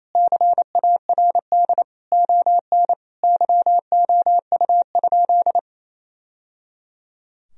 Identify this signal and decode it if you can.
Morse code, heard as a keyed tone, about 27 wpm, 700 Hz. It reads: CARD ON YOU?
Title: Sentences from Top 1000 Words - Review Part 4 - 28wpm